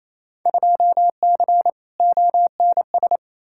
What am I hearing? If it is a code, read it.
Morse code, heard as a keyed tone, about 28 wpm, 700 Hz. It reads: 2C ONS
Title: Field Day Exchanges Encore - 28wpm